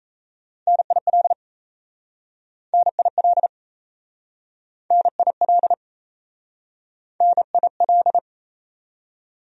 Morse code, heard as a keyed tone, about 38 wpm, 700 Hz. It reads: NIL NIL NIL NIL